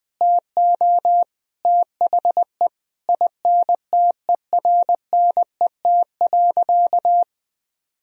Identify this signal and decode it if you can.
Morse code, heard as a keyed tone, about 20 wpm, 700 Hz. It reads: TO THE INTERNET.
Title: Sentences from Top 1000 Words - Review Part 4 - 20wpm